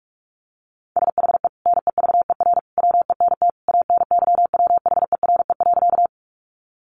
Morse code, heard as a keyed tone, about 44 wpm, 700 Hz. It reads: SHE NEVER WENT ANYWHERE.
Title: Sentences from Top 1400 Words 45wpm